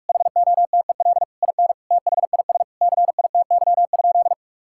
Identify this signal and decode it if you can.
Morse code, heard as a keyed tone, about 46 wpm, 700 Hz. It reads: HOTEL IN THIS CITY?
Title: Sentences from Top 1600 Words 45wpm